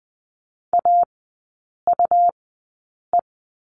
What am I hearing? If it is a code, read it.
Morse code, heard as a keyed tone, about 19 wpm, spaced as for 8 wpm, 700 Hz. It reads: AUE
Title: 191 Lesson 31e - Letter v - Three Characters 20wpm